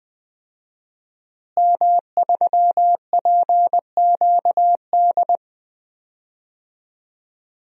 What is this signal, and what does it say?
Morse code, heard as a keyed tone, about 20 wpm, 700 Hz. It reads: M3PQD
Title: DXCC-100 Call Signs - 5 Characters - Repeated 3x - 20wpm